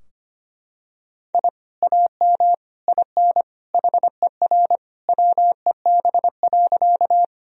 Morse code, heard as a keyed tone, about 25 wpm, 700 Hz. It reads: I AM IN HER WEB.
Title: Sentences from Top 1000 Words - Review Part 4 - 25wpm